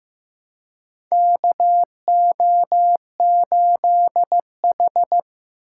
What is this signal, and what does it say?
Morse code, heard as a keyed tone, about 15 wpm, 700 Hz. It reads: KO8H
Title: USA Call Signs - 4 Characters - 4x Character Spacing - 15wpm